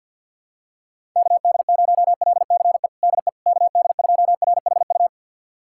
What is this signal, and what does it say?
Morse code, heard as a keyed tone, about 52 wpm, 700 Hz. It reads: KD0LCE DE KD2RHU
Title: USA Callsign DE Callsign - 6 Characters - 50wpm